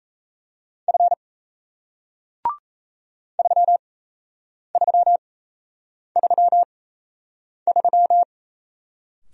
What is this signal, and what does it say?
Morse code, heard as a keyed tone, about 37 wpm, spaced as for 8 wpm, 700 Hz. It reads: F 3333